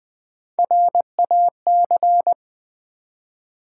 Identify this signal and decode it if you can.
Morse code, heard as a keyed tone, about 20 wpm, 700 Hz. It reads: RAC
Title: Build A Word - 1 Letter At A Time - 4-Letter Words - 20wpm